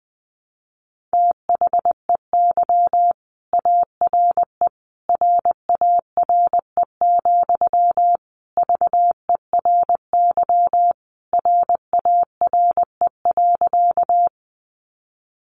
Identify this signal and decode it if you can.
Morse code, heard as a keyed tone, about 20 wpm, 700 Hz. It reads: THEY ARE RARE, VERY RARE.
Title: Sentences from Top 1100 Words 20wpm